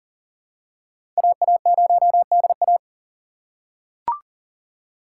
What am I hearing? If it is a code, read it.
Morse code, heard as a keyed tone, about 40 wpm, 700 Hz. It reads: AA0DA
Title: USA Call Signs - 5 Characters - 4x Character Spacing - 40wpm